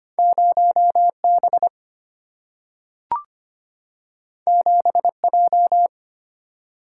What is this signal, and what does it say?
Morse code, heard as a keyed tone, about 25 wpm, 700 Hz. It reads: 0B 7J
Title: Single-Letter Single-Number 25wpm